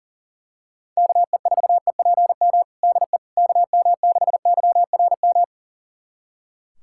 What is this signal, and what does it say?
Morse code, heard as a keyed tone, about 40 wpm, 700 Hz. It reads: KE4EPM DE KM6YRM
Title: USA Callsign DE Callsign - 6 Characters - 40wpm